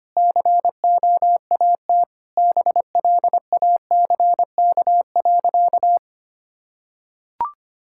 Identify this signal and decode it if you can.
Morse code, heard as a keyed tone, about 25 wpm, 700 Hz. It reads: COAT BLACK.
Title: Sentences from Top 1600 Words 25wpm